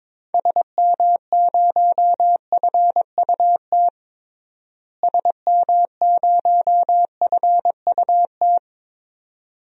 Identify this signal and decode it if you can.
Morse code, heard as a keyed tone, about 22 wpm, 700 Hz. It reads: SM0FUT SM0FUT
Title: DXCC-100 Call Signs - 6 Characters - Repeated 2x - 22wpm